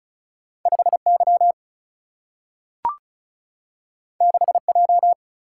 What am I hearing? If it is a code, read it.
Morse code, heard as a keyed tone, about 36 wpm, 700 Hz. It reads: 5 Y 6 J
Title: Single-Letter Single-Number 35wpm